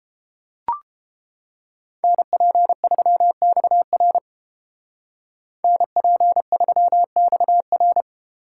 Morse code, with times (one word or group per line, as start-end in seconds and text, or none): 2.0-4.2 NP3XR
5.6-8.0 NP3XR